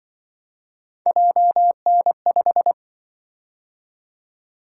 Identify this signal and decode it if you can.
Morse code, heard as a keyed tone, about 24 wpm, 700 Hz. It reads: JN5
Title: Speed-Racing - Three Letter-Number Combinations - 20wpm